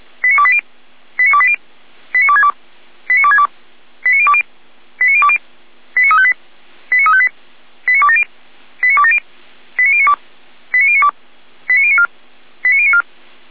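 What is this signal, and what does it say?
Five-tone selective call (ZVEI), heard as quick runs of five tones, tones of about 70 ms. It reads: 89280 89280 89262 89262 89020 89020 89358 89358 89270 89270 89092 89092 89094 89094
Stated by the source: ZVEI five-tone call 89050 received on an SDR.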